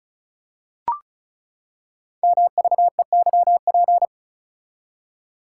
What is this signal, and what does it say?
Morse code, heard as a keyed tone, about 36 wpm, 700 Hz. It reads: MVEYP
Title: Code Groups - 5 Letters - 35wpm